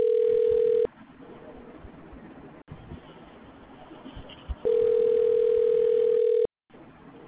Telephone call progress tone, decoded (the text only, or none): ringback tone